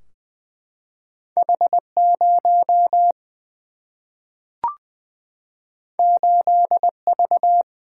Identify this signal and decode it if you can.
Morse code, heard as a keyed tone, about 20 wpm, 700 Hz. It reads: H0 8V
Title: Single-Letter Single-Number 20wpm